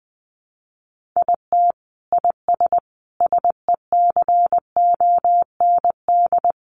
Morse code, heard as a keyed tone, about 19 wpm, 700 Hz. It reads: IT IS SECOND
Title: Sentences from Top 600 Words 20wpm